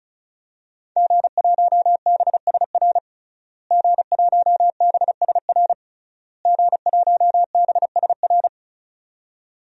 Morse code, heard as a keyed tone, about 34 wpm, 700 Hz. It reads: G1BSR G1BSR G1BSR
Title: DXCC-100 Call Signs - 5 Characters - Repeated 3x - 35wpm